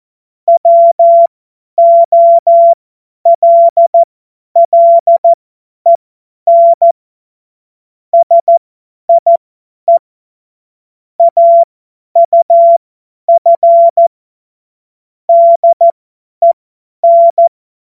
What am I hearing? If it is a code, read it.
Morse code, heard as a keyed tone, about 14 wpm, spaced as for 10 wpm, 680 Hz. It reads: WOLLEN SIE AUF DEN
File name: Morsekurs / Witze / Witz 25. April 2023 50 F70